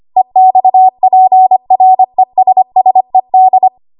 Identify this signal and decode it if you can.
Morse code, heard as a keyed tone, about 25 wpm, 750 Hz. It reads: EXPRESSED
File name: Archive / 25 WPM / 25WPM